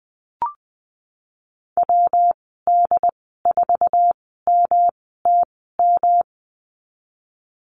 Morse code, heard as a keyed tone, about 19 wpm, spaced as for 14 wpm, 700 Hz. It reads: WD4MTM
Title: 122 Lesson 20g - Number 3 - Call Signs 20wpm